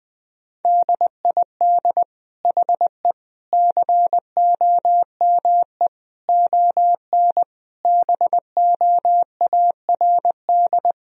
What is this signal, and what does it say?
Morse code, heard as a keyed tone, about 21 wpm, 700 Hz. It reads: DID HE COME ON BOARD
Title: Sentences from Top 1000 Words - Review Part 4 - 20wpm